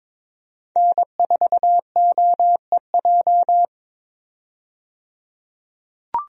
Morse code, heard as a keyed tone, about 22 wpm, 700 Hz. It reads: N4OEJ
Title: USA Call Signs - 5 Characters - 22wpm